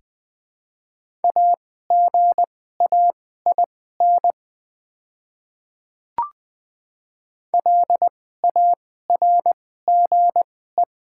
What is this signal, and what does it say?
Morse code, heard as a keyed tone, about 20 wpm, spaced as for 14 wpm, 700 Hz. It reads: AGAIN LARGE
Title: 2x Character Spacing - Top 200 Words - 20wpm